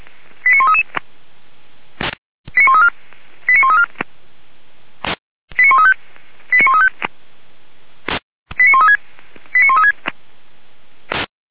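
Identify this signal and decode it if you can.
Five-tone selective call (ZVEI), heard as quick runs of five tones, tones of about 70 ms. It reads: 89133 89134 89134 89135 89135 89136 89136
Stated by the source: ZVEI five-tone call 89132 received by SDR, Feuerwehr Orlamünde.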